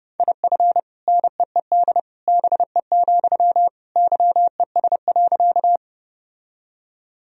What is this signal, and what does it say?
Morse code, heard as a keyed tone, about 30 wpm, 700 Hz. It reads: IF NEED BE, YES.
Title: Sentences from Top 1000 Words - Review Part 1 - 30wpm